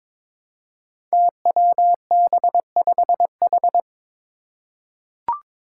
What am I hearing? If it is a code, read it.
Morse code, heard as a keyed tone, about 22 wpm, 700 Hz. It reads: TWB5H
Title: Five Challenging Letter-Number Combinations 22wpm